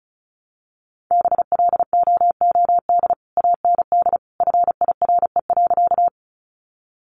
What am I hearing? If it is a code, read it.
Morse code, heard as a keyed tone, about 34 wpm, 700 Hz. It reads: BLOOD AND FIRE.
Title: Sentences from Top 1100 Words 35wpm